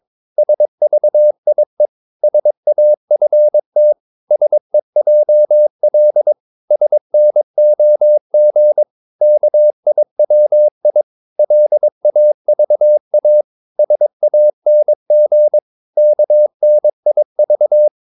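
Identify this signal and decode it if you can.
Morse code, heard as a keyed tone, about 22 wpm, 600 Hz. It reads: SVIE SAFT SEJL SNOG KIWI LAVA SANG KNIV
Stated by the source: Morse practice of short Danish words at 22 wpm, from CW med Gnister.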